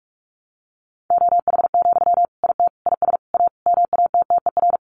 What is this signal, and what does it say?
Morse code, heard as a keyed tone, about 44 wpm, 700 Hz. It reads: OH, IT IS A MATTER
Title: Sentences from Top 1200 Words 45wpm